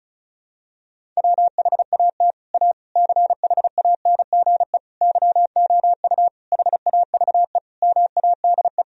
Morse code, heard as a keyed tone, about 36 wpm, 700 Hz. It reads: WHAT A CHANGE YOU HAVE MADE
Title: Sentences from Top 1000 Words - Review Part 1 - 35wpm